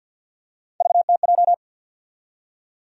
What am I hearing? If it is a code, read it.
Morse code, heard as a keyed tone, about 46 wpm, 700 Hz. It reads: VTJ
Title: USA Call Sign Postfixes - 3 Letters 50wpm